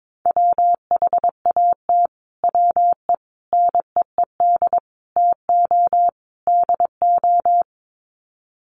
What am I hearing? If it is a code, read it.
Morse code, heard as a keyed tone, about 22 wpm, 700 Hz. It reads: WHAT WE NEED TO DO
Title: Sets of 5 Words Encore 22wpm